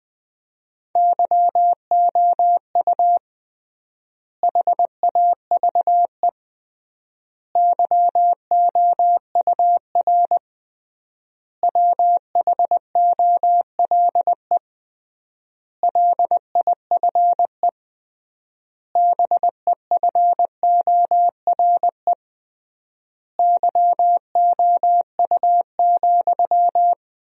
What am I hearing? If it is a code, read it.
Morse code, heard as a keyed tone, about 20 wpm, 700 Hz. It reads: YOU HAVE YOUR WHOLE LIFE BEFORE YOU,
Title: 3x Word-Spacing - Sentences from Top 400 Words - 20wpm